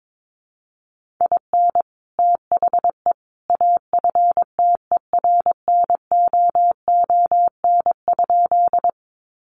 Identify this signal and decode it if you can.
Morse code, heard as a keyed tone, about 22 wpm, 700 Hz. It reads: IN THE AFTERNOON?
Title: Sentences from Top 1500 Words 22wpm